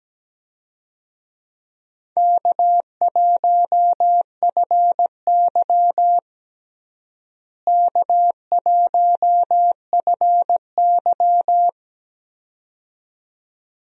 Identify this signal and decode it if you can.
Morse code, heard as a keyed tone, about 17 wpm, 700 Hz. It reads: K1FY K1FY
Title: USA Call Signs - 4 Characters - Repeated 2x - 17wpm